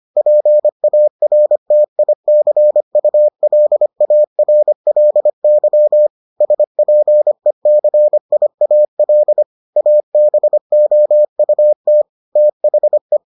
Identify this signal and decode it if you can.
Morse code, heard as a keyed tone, about 25 wpm, 600 Hz. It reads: PARTICULARLY SPECIAL ABOUT THE